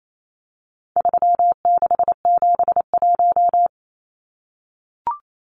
Morse code, heard as a keyed tone, about 28 wpm, 700 Hz. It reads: 3671
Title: Numbers - 4-Digit 28wpm